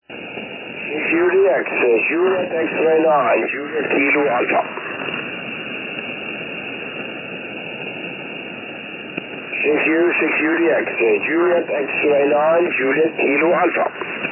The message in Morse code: none